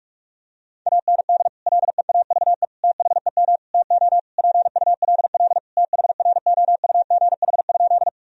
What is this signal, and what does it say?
Morse code, heard as a keyed tone, about 46 wpm, 700 Hz. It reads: AND LEAVE THEM TO PULL THROUGH?
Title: Sentences from Top 1000 Words - Review Part 2 - 45wpm